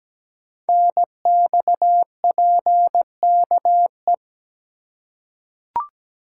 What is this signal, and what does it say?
Morse code, heard as a keyed tone, about 17 wpm, 700 Hz. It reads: NXPKE
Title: Code Groups - 5 Letters - 17wpm